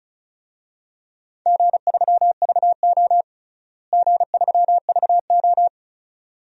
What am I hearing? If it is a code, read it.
Morse code, heard as a keyed tone, about 34 wpm, 700 Hz. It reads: G3VO G3VO
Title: DXCC-100 Call Signs - 4 Characters - Repeated 2x - 35wpm